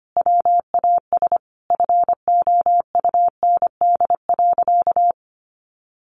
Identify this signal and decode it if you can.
Morse code, heard as a keyed tone, about 25 wpm, 700 Hz. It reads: WAS FOUND.
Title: Sentences from Top 1500 Words 25wpm